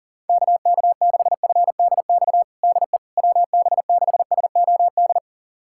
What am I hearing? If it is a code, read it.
Morse code, heard as a keyed tone, about 40 wpm, 700 Hz. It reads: KK6FDX DE WB6SOD
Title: USA Callsign DE Callsign - 6 Characters - 40wpm